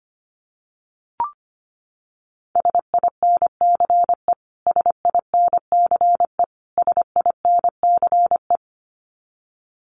Morse code, heard as a keyed tone, about 25 wpm, 700 Hz. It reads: SINCE SINCE SINCE
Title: Top 500 Words Repeated 25wpm